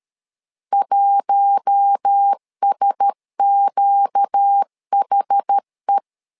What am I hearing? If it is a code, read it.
Morse code, heard as a keyed tone, about 13 wpm, 800 Hz. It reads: 1SQHE
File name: LEZIONI / 5 LBVC / 5 L 12wpm 1913